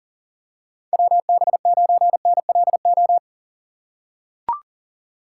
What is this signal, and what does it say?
Morse code, heard as a keyed tone, about 40 wpm, 700 Hz. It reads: WB9NLO
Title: USA Call Signs - 6 Characters - 4x Character Spacing - 40wpm